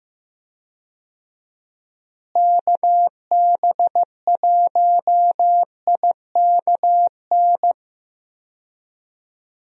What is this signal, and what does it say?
Morse code, heard as a keyed tone, about 15 wpm, 700 Hz. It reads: KB1IKN
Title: USA Call Signs - 6 Characters - Repeated 2x - 15wpm